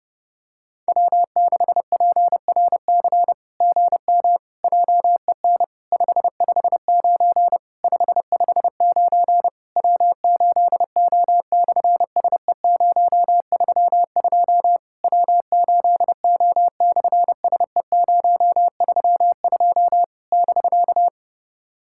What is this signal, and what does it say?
Morse code, heard as a keyed tone, about 29 wpm, 700 Hz. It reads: W6PRC GM JEN 559 559 W8O/SE032 W8O/SE032 <BK>
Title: SOTA QSO Practice 30wpm